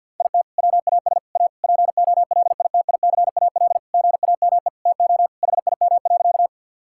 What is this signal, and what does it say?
Morse code, heard as a keyed tone, about 50 wpm, 700 Hz. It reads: IT WAS A POLITICAL GAME TO HIM.
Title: Sentences from Top 1000 Words - Review Part 5 - 50wpm